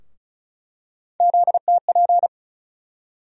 Morse code, heard as a keyed tone, about 36 wpm, 700 Hz. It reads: ZTP
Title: USA Call Sign Postfixes - 3 Letters 35wpm